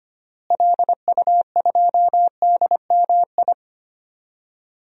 Morse code, heard as a keyed tone, about 25 wpm, 700 Hz. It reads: LU2DMI
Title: DXCC-100 Call Signs - 6 Characters - Repeated 3x - 25wpm